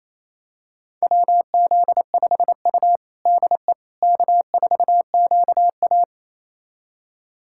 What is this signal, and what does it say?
Morse code, heard as a keyed tone, about 28 wpm, 700 Hz. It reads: WZ5U DE K4QA
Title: USA Callsign DE Callsign - 4 Characters - 28wpm